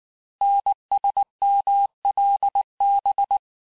Morse code, heard as a keyed tone, about 19 wpm, 800 Hz. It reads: NSMLB
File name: cours-du-lundi files / cours-1583--texte-chiffre-18-mpm